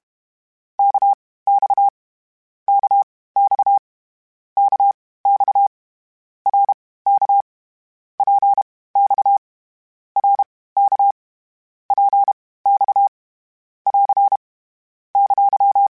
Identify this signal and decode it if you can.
Morse code, heard as a keyed tone, about 32 wpm, spaced as for 18 wpm, 800 Hz. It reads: KX KX KX RK PX RK PX <AR> !
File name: Short Runs / Short Runs-mp3s / Balanced @30